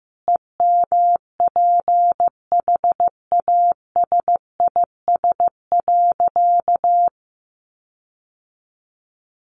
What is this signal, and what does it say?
Morse code, heard as a keyed tone, about 15 wpm, 700 Hz. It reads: EMPHASIS.
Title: Sentences from Top 1400 Words 15wpm